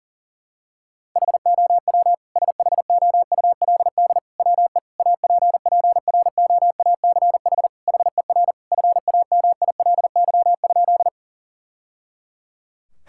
Morse code, heard as a keyed tone, about 40 wpm, 700 Hz. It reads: HOW SHOULD WE APPROACH HER FAMILY?
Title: Sentences from Top 1000 Words - Review Part 6 - 40wpm